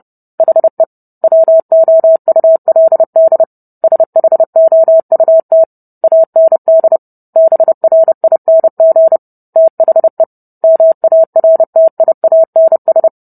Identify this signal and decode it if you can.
Morse code, heard as a keyed tone, about 30 wpm, 650 Hz. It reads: HE WOULD SHOUT AND BRING THE MARTIANS